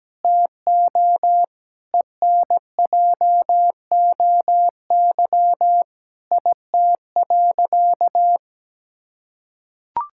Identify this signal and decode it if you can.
Morse code, heard as a keyed tone, about 17 wpm, 700 Hz. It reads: TO ENJOY IT.
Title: Sentences from Top 1000 Words - Review Part 5 - 17wpm